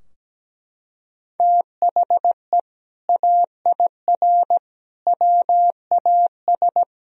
Morse code, heard as a keyed tone, about 17 wpm, 700 Hz. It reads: THE AIR WAS
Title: Sentences from Top 1000 Words - Review Part 3 - 17wpm